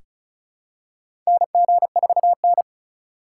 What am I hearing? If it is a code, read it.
Morse code, heard as a keyed tone, about 36 wpm, 700 Hz. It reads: NG4N
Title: USA Call Signs - 3x Character Spacing - 35wpm